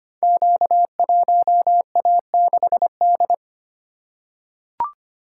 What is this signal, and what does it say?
Morse code, heard as a keyed tone, about 25 wpm, 700 Hz. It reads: Q1A6D
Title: Code Groups - 5 Letter:Numbers - 25wpm